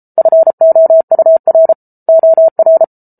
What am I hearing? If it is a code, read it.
Morse code, heard as a keyed tone, about 34 wpm, 650 Hz. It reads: FOUR OR